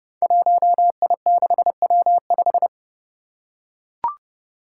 Morse code, heard as a keyed tone, about 30 wpm, 700 Hz. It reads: 1I6W5
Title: Five Challenging Letter-Number Combinations 30wpm